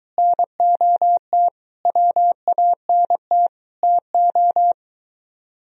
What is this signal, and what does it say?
Morse code, heard as a keyed tone, about 22 wpm, 700 Hz. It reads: NOT WANT TO